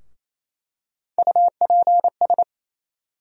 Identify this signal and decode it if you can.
Morse code, heard as a keyed tone, about 28 wpm, 700 Hz. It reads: UPS
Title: USA Call Sign Postfixes - 3 Letters 28wpm